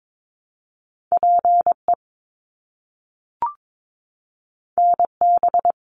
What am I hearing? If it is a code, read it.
Morse code, heard as a keyed tone, about 22 wpm, 700 Hz. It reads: PE NB
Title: Canadian Provinces and Territories - Repeated 3x - 22wpm